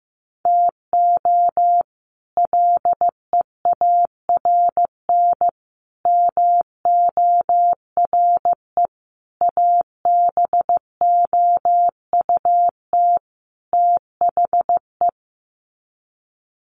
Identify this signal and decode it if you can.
Morse code, heard as a keyed tone, about 15 wpm, 700 Hz. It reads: TO LEARN MORE ABOUT THE
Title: Sets of 5 Words Encore2 15wpm